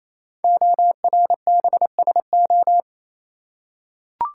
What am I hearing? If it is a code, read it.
Morse code, heard as a keyed tone, about 28 wpm, 700 Hz. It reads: ORBSO
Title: Code Groups - 5 Letter:Numbers - 28wpm